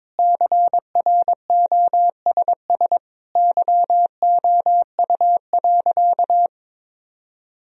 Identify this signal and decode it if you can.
Morse code, heard as a keyed tone, about 21 wpm, 700 Hz. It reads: CROSS YOU.
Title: Sentences from Top 1000 Words - Review Part 1 - 22wpm